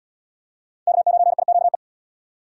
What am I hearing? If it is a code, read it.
Morse code, heard as a keyed tone, about 65 wpm, 700 Hz. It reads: K9IQE